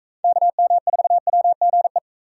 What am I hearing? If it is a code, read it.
Morse code, heard as a keyed tone, about 40 wpm, 700 Hz. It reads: KM4WGE